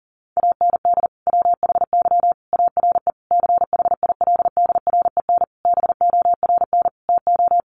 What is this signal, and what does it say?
Morse code, heard as a keyed tone, about 37 wpm, 700 Hz. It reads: AND WHY ARE CHILDREN BORN TO
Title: Sentences from Top 1000 Words 40wpm